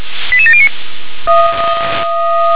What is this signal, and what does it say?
Five-tone selective call (ZVEI), heard as quick runs of five tones, tones of about 70 ms. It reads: 88709